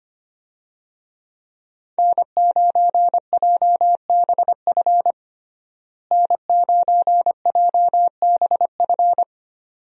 Morse code, heard as a keyed tone, about 25 wpm, 700 Hz. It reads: N9JBF N9JBF
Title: USA Call Signs - 5 Characters - Repeated 2x - 25wpm